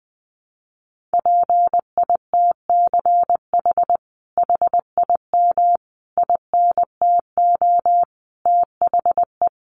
Morse code, heard as a keyed tone, about 20 wpm, 700 Hz. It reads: PITCH HIM INTO THE
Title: Sentences from Top 1300 Words 20wpm